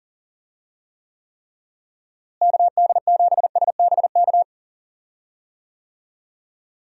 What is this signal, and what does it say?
Morse code, heard as a keyed tone, about 40 wpm, 700 Hz. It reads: KD7SBK
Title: USA Call Signs - 6 Characters - Repeated 3x - 40wpm